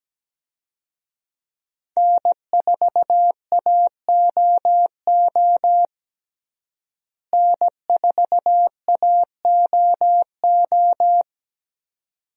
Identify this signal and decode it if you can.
Morse code, heard as a keyed tone, about 17 wpm, 700 Hz. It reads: N4AOO N4AOO
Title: USA Call Signs - 5 Characters - Repeated 2x - 17wpm